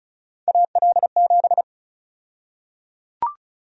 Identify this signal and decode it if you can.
Morse code, heard as a keyed tone, about 34 wpm, 700 Hz. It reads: AL7
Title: USA Call Sign Prefixes - 35wpm